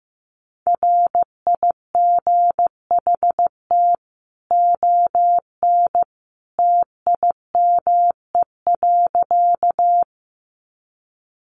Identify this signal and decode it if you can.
Morse code, heard as a keyed tone, about 15 wpm, 700 Hz. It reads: RIGHT ON TIME.
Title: A Sound Of Thunder 15wpm